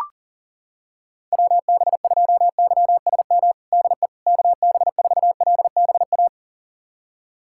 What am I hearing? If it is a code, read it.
Morse code, heard as a keyed tone, about 40 wpm, 700 Hz. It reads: WB2YSM DE KB4LBA
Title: USA Callsign DE Callsign 40wpm